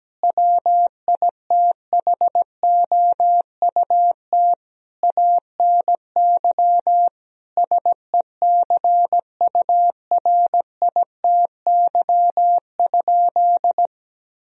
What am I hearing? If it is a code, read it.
Morse code, heard as a keyed tone, about 17 wpm, 700 Hz. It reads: WITHOUT ANY SECURITY?
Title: Sentences from Top 1000 Words - Review Part 4 - 17wpm